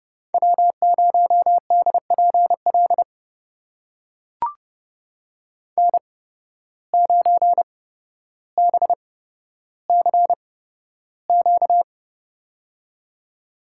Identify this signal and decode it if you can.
Morse code, heard as a keyed tone, about 31 wpm, spaced as for 13 wpm, 700 Hz. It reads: W0DPL N9BCQ
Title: USA Call Signs - 8x Character Spacing - 30wpm